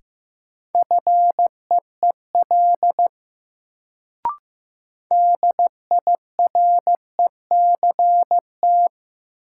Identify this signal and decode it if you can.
Morse code, heard as a keyed tone, about 16 wpm, 700 Hz. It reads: FEEL DIRECT
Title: Sentences from Top 400 Words - New Word Drill - 15wpm